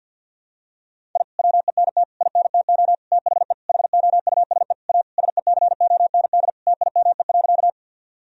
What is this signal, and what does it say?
Morse code, heard as a keyed tone, about 47 wpm, 700 Hz. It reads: I WENT INTO THE HOUSE A SECOND TIME.
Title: Sentences from Top 1000 Words - Review Part 1 - 50wpm